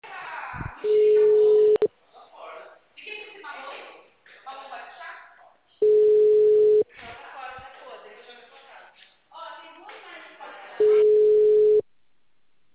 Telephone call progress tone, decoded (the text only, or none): ringback tone